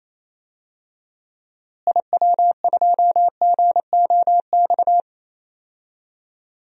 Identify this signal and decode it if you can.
Morse code, heard as a keyed tone, about 27 wpm, 700 Hz. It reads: IW2GOX
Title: DXCC-100 Call Signs - 6 Characters - Repeated 2x - 28wpm